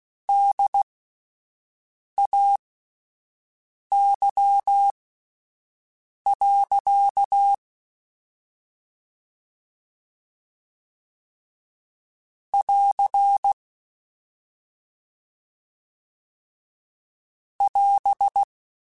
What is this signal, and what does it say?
Morse code, heard as a keyed tone, about 16 wpm, spaced as for 5 wpm, 800 Hz. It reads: DAY. <AR> <AS>
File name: Spaced Characters / Spaced Characters-mp3s / Weather5 12 sp @10